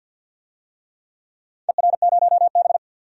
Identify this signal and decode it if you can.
Morse code, heard as a keyed tone, about 46 wpm, 700 Hz. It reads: ER0B
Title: DXCC-100 Call Signs - 4 Characters - Repeated 2x - 50wpm